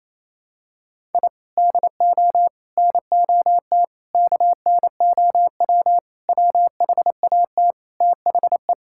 Morse code, heard as a keyed tone, about 27 wpm, 700 Hz. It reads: I DO NOT KNOW WHAT THE
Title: Sentences from Top 1000 Words - Review Part 6 - 28wpm